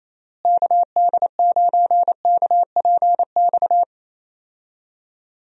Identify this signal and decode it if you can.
Morse code, heard as a keyed tone, about 28 wpm, 700 Hz. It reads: KD9KPX
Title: USA Call Signs - 6 Characters - Repeated 3x - 28wpm